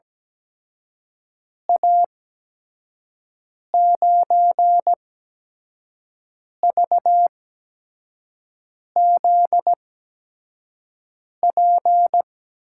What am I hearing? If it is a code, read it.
Morse code, heard as a keyed tone, about 17 wpm, spaced as for 5 wpm, 700 Hz. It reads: A9VZP
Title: USA Call Signs - 6 Characters - 8x Character Spacing - 17wpm